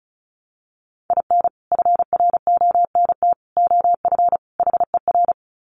Morse code, heard as a keyed tone, about 34 wpm, 700 Hz. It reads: IN FRONT OF HER
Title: Sets of 4 Words 35wpm